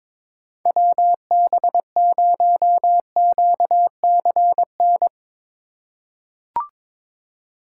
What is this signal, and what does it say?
Morse code, heard as a keyed tone, about 21 wpm, 700 Hz. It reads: WB0QCN
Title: USA Call Signs - 6 Characters - 2x Character Spacing - 22wpm